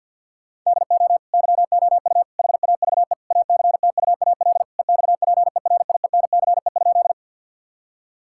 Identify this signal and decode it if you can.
Morse code, heard as a keyed tone, about 50 wpm, 700 Hz. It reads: DO YOU HAVE ACTUAL EXPERIENCE?
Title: Sentences from Top 1000 Words - Review Part 5 - 50wpm